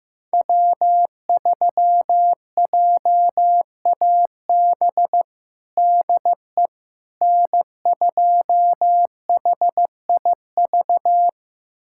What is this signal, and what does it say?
Morse code, heard as a keyed tone, about 16 wpm, 700 Hz. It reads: W3JAB DE N2HIV
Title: USA Callsign DE Callsign - 5 Characters - 15wpm